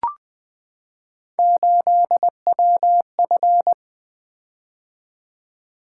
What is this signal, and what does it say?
Morse code, heard as a keyed tone, about 20 wpm, 700 Hz. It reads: 8WF
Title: Three Letter-Number Combinations - 20wpm